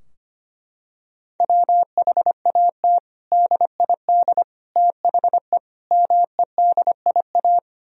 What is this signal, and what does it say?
Morse code, heard as a keyed tone, about 25 wpm, 700 Hz. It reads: WHAT DID THE MEDIA
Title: Sentences from Top 1000 Words - Review Part 4 - 25wpm